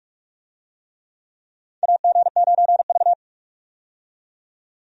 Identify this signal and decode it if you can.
Morse code, heard as a keyed tone, about 42 wpm, 700 Hz. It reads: AG9V
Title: USA Call Signs - 4 Characters - 45wpm